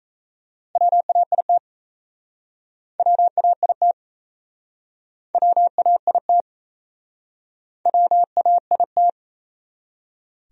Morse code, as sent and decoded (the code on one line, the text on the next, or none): .-- .- .. - / .-- .- .. - / .-- .- .. - / .-- .- .. -
WAIT WAIT WAIT WAIT